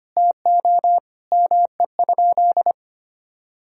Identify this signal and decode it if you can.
Morse code, heard as a keyed tone, about 25 wpm, 700 Hz. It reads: TO ME?
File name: practice / 25wpm / Sentences with 2-Word Phrases - 25wpm